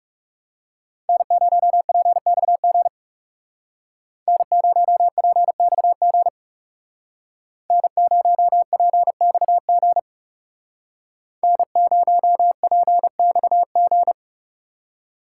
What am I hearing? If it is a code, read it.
Morse code, heard as a keyed tone, about 35 wpm, 700 Hz. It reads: N0PXG N0PXG N0PXG N0PXG
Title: Speed Racing - USA 1x3 Call Signs 30wpm